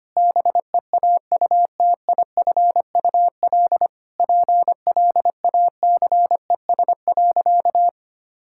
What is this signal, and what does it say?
Morse code, heard as a keyed tone, about 25 wpm, 700 Hz. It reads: BEAUTIFUL PLACES.
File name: practice / 25wpm / Sentences with 3-Word Phrases - 25wpm